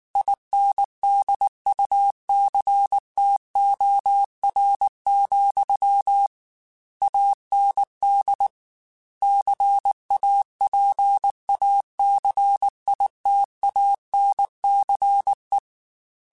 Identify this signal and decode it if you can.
Morse code, heard as a keyed tone, about 19 wpm, 800 Hz. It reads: INDUCTOR, AND CAPACITANCE
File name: Technical Subjects / Technical Subjects-mp3s / Reactance @18